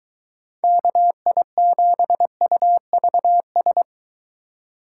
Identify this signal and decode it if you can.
Morse code, heard as a keyed tone, about 22 wpm, 700 Hz. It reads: KI7UVS